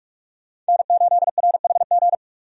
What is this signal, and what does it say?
Morse code, heard as a keyed tone, about 46 wpm, 700 Hz. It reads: N8RHG
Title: USA Call Signs - 5 Characters - 45wpm